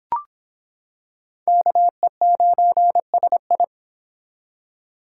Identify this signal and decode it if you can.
Morse code, heard as a keyed tone, about 26 wpm, 700 Hz. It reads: KE9SI